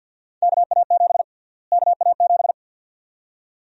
Morse code, heard as a keyed tone, about 52 wpm, 700 Hz. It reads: KA7 KA7
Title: USA Call Sign Prefixes - Repeated 2x - 50wpm